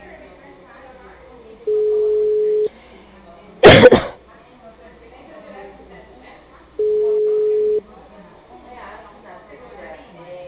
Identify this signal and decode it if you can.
Ringback tone (Europe, 425 Hz), with speech.